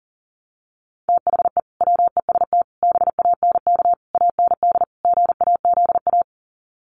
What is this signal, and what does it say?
Morse code, heard as a keyed tone, about 40 wpm, 700 Hz. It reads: THE WEST BANK AND GAZA
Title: Sets of 5 Words 40wpm